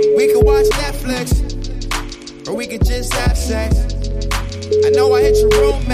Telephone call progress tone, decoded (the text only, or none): ringback tone